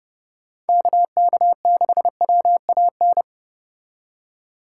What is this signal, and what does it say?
Morse code, heard as a keyed tone, about 29 wpm, 700 Hz. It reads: KK6WAN